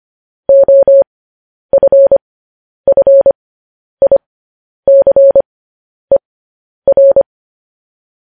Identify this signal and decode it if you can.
Morse code, heard as a keyed tone, about 25 wpm, spaced as for 10 wpm, 550 Hz. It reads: OFFICER
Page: news headlines in morse code at 10 wpm